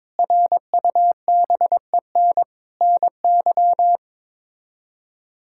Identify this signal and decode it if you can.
Morse code, heard as a keyed tone, about 22 wpm, 700 Hz. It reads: RUBEN NY
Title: K1USN SST QSO Practice - 22wpm